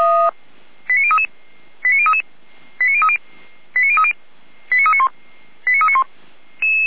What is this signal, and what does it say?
Five-tone selective call (ZVEI), heard as quick runs of five tones, tones of about 70 ms. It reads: 89033 89033 89030 89030 89391 89391
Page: SDR/ZVEI